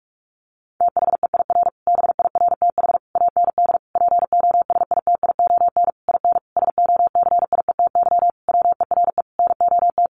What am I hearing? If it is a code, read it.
Morse code, heard as a keyed tone, about 44 wpm, 700 Hz. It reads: THEIR BIRTH AND POSITION IN SOCIETY WERE NOT
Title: Sentences from Top 900 Words 45wpm